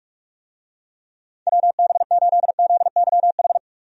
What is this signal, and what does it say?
Morse code, heard as a keyed tone, about 46 wpm, 700 Hz. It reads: WB8ZYH
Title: USA Call Signs - 6 Characters - 45wpm